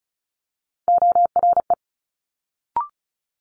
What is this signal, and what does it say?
Morse code, heard as a keyed tone, about 33 wpm, 700 Hz. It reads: ORE
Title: Multi-letter Phonemes Quick Fire 35wpm